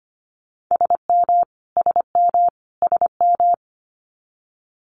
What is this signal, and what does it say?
Morse code, heard as a keyed tone, about 25 wpm, 700 Hz. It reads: SM SM SM
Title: Multi-letter Phonemes Repeated 3x 25wpm